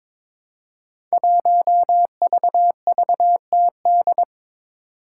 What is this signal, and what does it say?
Morse code, heard as a keyed tone, about 23 wpm, 700 Hz. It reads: 1VVTD
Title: Five Challenging Letter-Number Combinations 22wpm